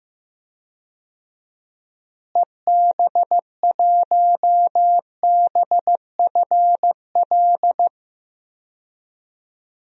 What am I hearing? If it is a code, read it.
Morse code, heard as a keyed tone, about 15 wpm, 700 Hz. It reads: EB1BFL
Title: DXCC-100 Call Signs - 6 Characters - Repeated 2x - 15wpm